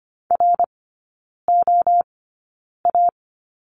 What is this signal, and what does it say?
Morse code, heard as a keyed tone, about 25 wpm, spaced as for 9 wpm, 700 Hz. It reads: ROA
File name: course / 25:10wpm / 100 Lesson 17e - Letter u - Three Characters 25wpm